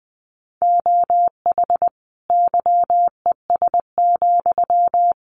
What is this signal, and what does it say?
Morse code, heard as a keyed tone, about 20 wpm, 700 Hz. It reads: OH YES,
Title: Sentences from Top 1300 Words 20wpm